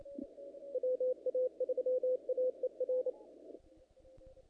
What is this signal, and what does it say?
Morse code, heard as a keyed tone, about 30 wpm, 500 Hz. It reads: WA3AER
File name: contests / naqp-cw-k5tr-op-k5pi- -1900